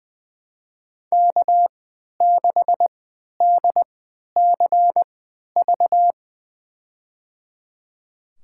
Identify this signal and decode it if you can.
Morse code, heard as a keyed tone, about 20 wpm, spaced as for 11 wpm, 700 Hz. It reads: K6DCV
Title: USA Call Signs - 5 Characters - 3x Character Spacing - 20wpm